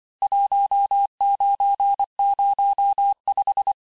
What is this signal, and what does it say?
Morse code, heard as a keyed tone, about 25 wpm, 800 Hz. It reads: 1905
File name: cours-du-lundi files / cours-1589-texte-clair-vitesse-18mpm-puis-23mpm